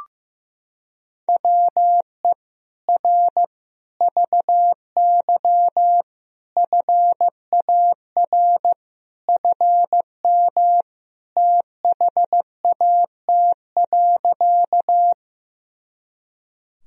Morse code, heard as a keyed tone, about 15 wpm, 700 Hz. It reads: WE R VY FAR FM THAT.
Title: Sentences from Top 300 Words and Abbreviations 15wpm